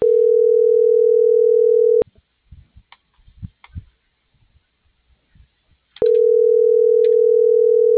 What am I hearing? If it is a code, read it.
Ringback tone (North America), 440 + 480 Hz.
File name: CENTRAL MAERSK / out adolfo.msk - .29857